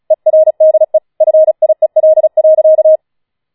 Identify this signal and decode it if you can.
Morse code, heard as a keyed tone, about 36 wpm, 630 Hz. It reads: ERDE FIEL.
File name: Schneewittchen Teil6 36wpm